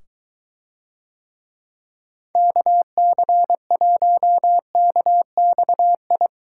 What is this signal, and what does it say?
Morse code, heard as a keyed tone, about 22 wpm, 700 Hz. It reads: KC1KXI